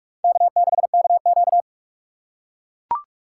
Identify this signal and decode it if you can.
Morse code, heard as a keyed tone, about 43 wpm, 700 Hz. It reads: K6KQ
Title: USA Call Signs - 4 Characters - 6x Character Spacing - 45wpm